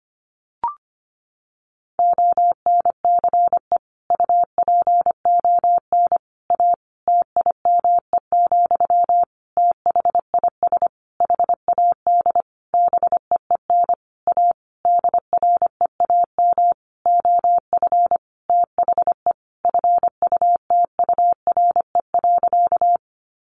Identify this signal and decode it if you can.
Morse code, heard as a keyed tone, about 25 wpm, 700 Hz. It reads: ONCE UPON A TIME, THIS HAD BEEN A DREAM OF THE FUTURE.
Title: Sentences from Top 1500 Words 25wpm